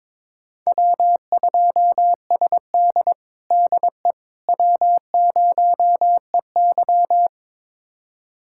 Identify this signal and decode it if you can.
Morse code, heard as a keyed tone, about 22 wpm, 700 Hz. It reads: W2SD DE W0EY
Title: USA Callsign DE Callsign - 4 Characters - 22wpm